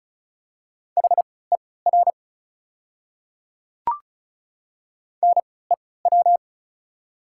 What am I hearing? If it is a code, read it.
Morse code, heard as a keyed tone, about 36 wpm, spaced as for 20 wpm, 700 Hz. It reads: HER NEW
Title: 3x Character Spacing - Top 200 Words - 35wpm